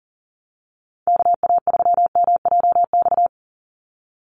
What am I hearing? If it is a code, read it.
Morse code, heard as a keyed tone, about 40 wpm, 700 Hz. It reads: KA3MJX
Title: USA Call Signs 40wpm